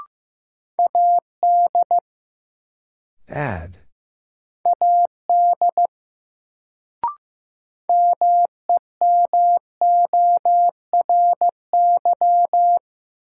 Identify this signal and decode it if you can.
Morse code, heard as a keyed tone, about 15 wpm, 700 Hz. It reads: AD AD MEMORY
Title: Sentences from Top 1300 Words - New Word Drill - 15wpm